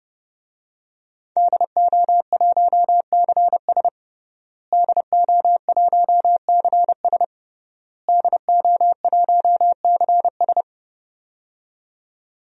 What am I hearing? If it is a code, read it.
Morse code, heard as a keyed tone, about 30 wpm, 700 Hz. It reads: DO1CS DO1CS DO1CS
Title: DXCC-100 Call Signs - 5 Characters - Repeated 3x - 30wpm